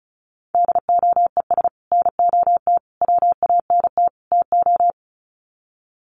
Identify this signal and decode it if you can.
Morse code, heard as a keyed tone, about 34 wpm, 700 Hz. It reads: DOES NOT WANT TO
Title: Sets of 4 Words 35wpm